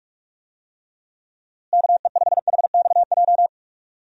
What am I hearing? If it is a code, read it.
Morse code, heard as a keyed tone, about 45 wpm, 700 Hz. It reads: KE5HXJ
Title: USA Call Signs - 6 Characters - Repeated 2x - 45wpm